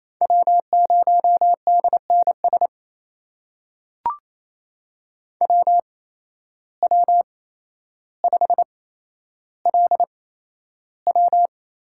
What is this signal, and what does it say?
Morse code, heard as a keyed tone, about 28 wpm, spaced as for 12 wpm, 700 Hz. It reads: W0DNS WW5LW
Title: USA Call Signs - 5 Characters - 8x Character Spacing - 28wpm